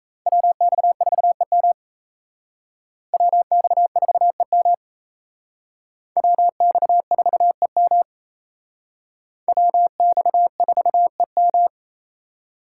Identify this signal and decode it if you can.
Morse code, heard as a keyed tone, about 35 wpm, 700 Hz. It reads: WX4EM WX4EM WX4EM WX4EM
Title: Speed Racing - USA 2x2 Call Signs 28wpm